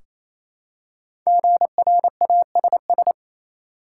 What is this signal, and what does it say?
Morse code, heard as a keyed tone, about 28 wpm, 700 Hz. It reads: GRASS
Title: Five Letter Words - 28wpm